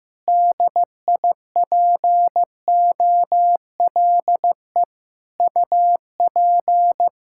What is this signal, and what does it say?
Morse code, heard as a keyed tone, about 15 wpm, 700 Hz. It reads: DIPOLE UP
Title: Practice QSO - Exchange 2 - Info - 15wpm